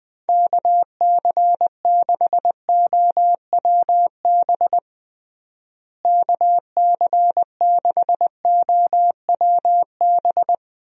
Morse code, heard as a keyed tone, about 20 wpm, 700 Hz. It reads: KC6OWB KC6OWB
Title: USA Call Signs - 6 Characters - Repeated 3x - 20wpm